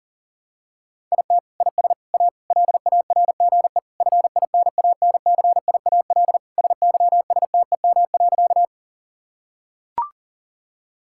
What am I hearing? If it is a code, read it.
Morse code, heard as a keyed tone, about 40 wpm, 700 Hz. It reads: IT IS A LARGE FINANCIAL SYSTEM.
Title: Sentences from Top 1000 Words - Review Part 5 - 40wpm